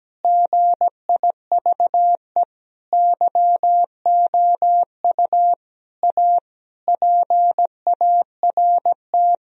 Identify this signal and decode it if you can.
Morse code, heard as a keyed tone, about 17 wpm, 700 Hz. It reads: GIVE YOU A PART
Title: Sentences from Top 1000 Words - Review Part 1 - 17wpm